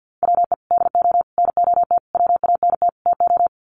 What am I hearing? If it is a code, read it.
Morse code, heard as a keyed tone, about 48 wpm, 700 Hz. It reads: WE DO NOT WANT TO
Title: Sets of 5 Words 50wpm